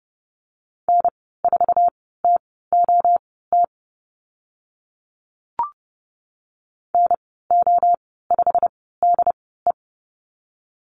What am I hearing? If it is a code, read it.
Morse code, heard as a keyed tone, about 30 wpm, spaced as for 17 wpm, 700 Hz. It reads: N4TOT NO5DE
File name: Lesson 15g - Number 5 - Call Signs 30wpm